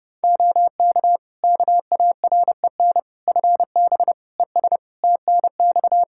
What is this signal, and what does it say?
Morse code, heard as a keyed tone, about 32 wpm, 700 Hz. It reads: OK KAREN FB ES TNX
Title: Practice QSO - Exchange 2 - Info - 30wpm